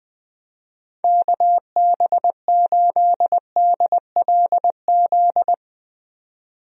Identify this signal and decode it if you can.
Morse code, heard as a keyed tone, about 20 wpm, 700 Hz. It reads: KB8DLZ
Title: USA Call Signs - 6 Characters - Repeated 3x - 20wpm